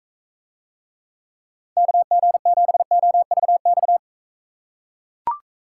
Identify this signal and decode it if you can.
Morse code, heard as a keyed tone, about 43 wpm, 700 Hz. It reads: KG7OVX